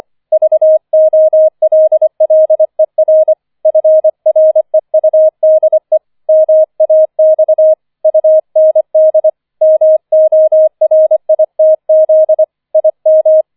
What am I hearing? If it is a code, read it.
Morse code, heard as a keyed tone, about 25 wpm, 630 Hz. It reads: VOLLER FREUDE MAX UND MORITZ IM